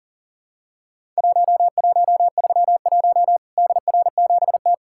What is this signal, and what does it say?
Morse code, heard as a keyed tone, about 40 wpm, 700 Hz. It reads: 1131 DR7T
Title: Worked All Europe DX Contest - QTC Exchange - 40wpm